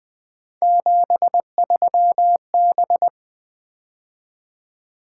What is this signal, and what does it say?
Morse code, heard as a keyed tone, about 20 wpm, 700 Hz. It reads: 73B
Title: Three Letter-Number Combinations - 20wpm